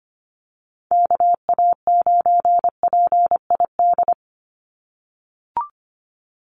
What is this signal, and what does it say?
Morse code, heard as a keyed tone, about 25 wpm, 700 Hz. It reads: KA9PID
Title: USA Call Signs 25wpm